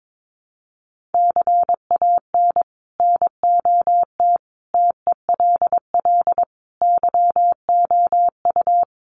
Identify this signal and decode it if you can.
Morse code, heard as a keyed tone, about 22 wpm, 700 Hz. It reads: CAN NOT TELL YOU
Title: Sets of 5 Words Encore2 22wpm